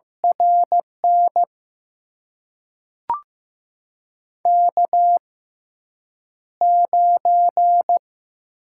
Morse code, heard as a keyed tone, about 15 wpm, spaced as for 8 wpm, 700 Hz. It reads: RN K9
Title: USA Call Signs - 5 Characters - 6x Character Spacing - 15wpm